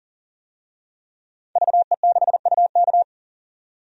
Morse code, heard as a keyed tone, about 40 wpm, 700 Hz. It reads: VE6UK
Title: DXCC-100 Call Signs - 5 Characters - 40wpm